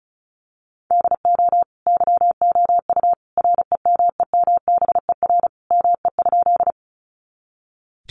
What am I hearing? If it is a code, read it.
Morse code, heard as a keyed tone, about 34 wpm, 700 Hz. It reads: DO YOU REMEMBER ME?
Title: Sentences from Top 400 Words 35wpm